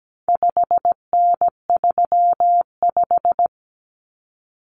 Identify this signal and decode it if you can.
Morse code, heard as a keyed tone, about 17 wpm, 700 Hz. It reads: 5N35
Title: Common Cut Numbers - 4-Digit 17wpm